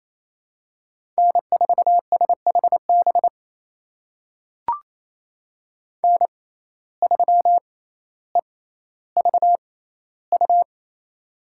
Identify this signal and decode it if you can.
Morse code, heard as a keyed tone, about 27 wpm, spaced as for 14 wpm, 700 Hz. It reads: N4SHB N3EVU
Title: USA Call Signs - 5 Characters - 6x Character Spacing - 28wpm